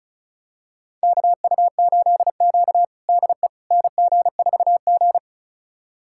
Morse code, heard as a keyed tone, about 34 wpm, 700 Hz. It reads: KU8Q DE NG4G
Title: USA Callsign DE Callsign - 4 Characters - 35wpm